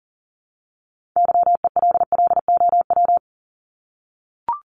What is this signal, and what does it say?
Morse code, heard as a keyed tone, about 40 wpm, 700 Hz. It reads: YELLOW
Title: Top 1751-2000 Words 40wpm